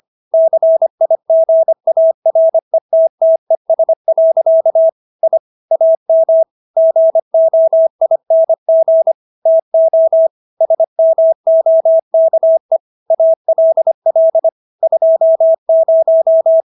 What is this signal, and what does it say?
Morse code, heard as a keyed tone, about 25 wpm, 650 Hz. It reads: CIGARETTES. I AM GOING TO SMOKE ALL 20